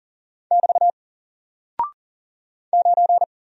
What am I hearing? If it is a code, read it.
Morse code, heard as a keyed tone, about 38 wpm, 700 Hz. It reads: <BT> 9
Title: Single Character (Letter:Number:Punctuation:Prosign) Alt 40wpm